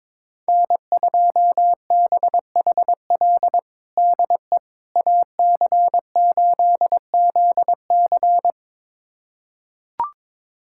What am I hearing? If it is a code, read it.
Morse code, heard as a keyed tone, about 22 wpm, 700 Hz. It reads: N2BHL DE AC8ZC
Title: USA Callsign DE Callsign - 5 Characters - 22wpm